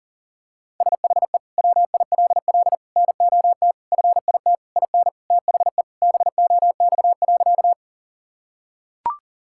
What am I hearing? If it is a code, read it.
Morse code, heard as a keyed tone, about 40 wpm, 700 Hz. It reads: SHE WILL NOT FIT IN THE BOX.
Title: Sentences from Top 1000 Words - Review Part 5 - 40wpm